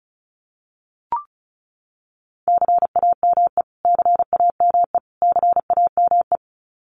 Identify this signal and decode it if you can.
Morse code, heard as a keyed tone, about 34 wpm, 700 Hz. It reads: CAME CAME CAME
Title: Top 300 Words Repeated 35wpm